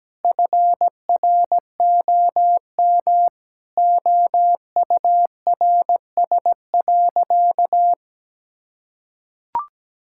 Morse code, heard as a keyed tone, about 17 wpm, 700 Hz. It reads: FROM OURS.
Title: Sentences from Top 1000 Words - Review Part 5 - 17wpm